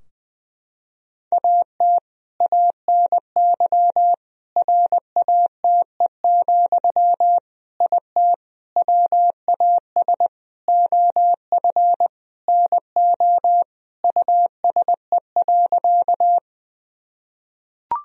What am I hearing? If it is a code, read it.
Morse code, heard as a keyed tone, about 21 wpm, 700 Hz. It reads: AT ANY RATE, IT WAS OF NO USE.
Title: Sentences from Top 1000 Words - Review Part 4 - 20wpm